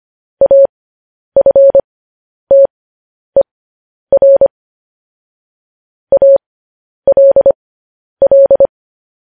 Morse code, heard as a keyed tone, about 25 wpm, spaced as for 10 wpm, 550 Hz. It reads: AFTER ALL
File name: podcast / News headlines-1x-25x10